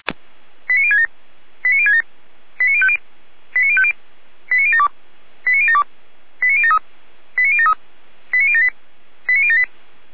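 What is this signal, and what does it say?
Five-tone selective call (ZVEI), heard as quick runs of five tones, tones of about 70 ms. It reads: 89067 89067 89050 89050 89072 89072 89073 89073 89078 89078
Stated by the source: ZVEI five-tone call 89123 received on SDR, Feuerwehr Orlamünde.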